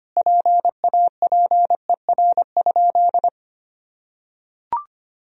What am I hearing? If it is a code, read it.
Morse code, heard as a keyed tone, about 25 wpm, 700 Hz. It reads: PAPER?
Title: Sentences from Top 1000 Words - Review Part 2 - 25wpm